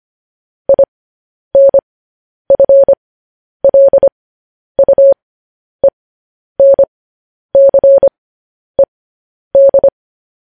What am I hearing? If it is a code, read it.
Morse code, heard as a keyed tone, about 25 wpm, spaced as for 10 wpm, 550 Hz. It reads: INFLUENCED
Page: news headlines in morse code at 10 wpm